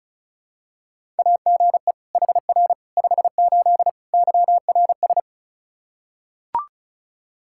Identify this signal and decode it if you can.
Morse code, heard as a keyed tone, about 36 wpm, 700 Hz. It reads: AGE HR 58 YRS
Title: Practice QSO - Exchange 3 - More Info - 35wpm